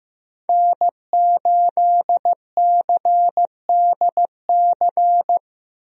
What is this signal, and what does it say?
Morse code, heard as a keyed tone, about 16 wpm, 700 Hz. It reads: N8CDC
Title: USA Call Signs - 5 Characters - Repeated 2x - 15wpm